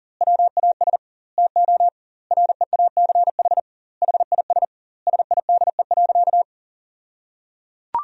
Morse code, heard as a keyed tone, about 37 wpm, 700 Hz. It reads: WAS TO REACH HIS SIDE.
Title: 2x Word-Spacing - Sentences from Top 400 Words - 40wpm